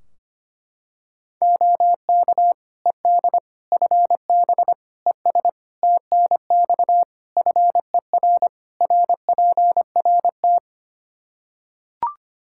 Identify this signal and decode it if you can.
Morse code, heard as a keyed tone, about 25 wpm, 700 Hz. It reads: OK ED FB ES TNX FER RPRT
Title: K1USN SST QSO Practice 25wpm